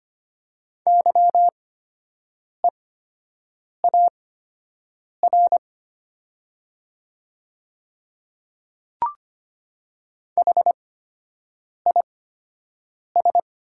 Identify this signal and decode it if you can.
Morse code, heard as a keyed tone, about 25 wpm, spaced as for 7 wpm, 700 Hz. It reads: YEAR HIS
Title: 8x Character Spacing - Top 300 Words - 25wpm